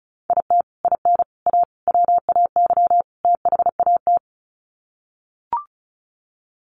Morse code, heard as a keyed tone, about 34 wpm, 700 Hz. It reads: IT IN A WAY THAT
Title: Sets of 5 Words Encore2 35wpm